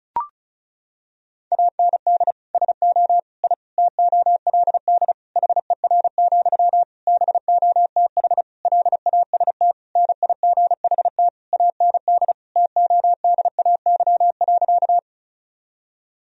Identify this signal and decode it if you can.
Morse code, heard as a keyed tone, about 34 wpm, 700 Hz. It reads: AND SO I TOLD HER, BOTH LAST NIGHT AND TODAY.
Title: Sentences from Top 1000 Words - Review Part 4 - 35wpm